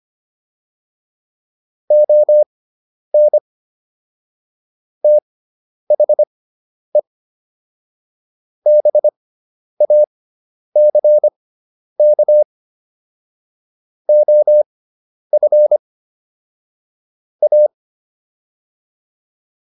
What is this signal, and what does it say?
Morse code, heard as a keyed tone, about 25 wpm, spaced as for 10 wpm, 600 Hz. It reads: ON THE BACK OF A